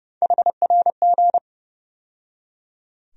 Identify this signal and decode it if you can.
Morse code, heard as a keyed tone, about 30 wpm, 700 Hz. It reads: HRG
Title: USA Call Sign Postfixes - 3 Letters 30wpm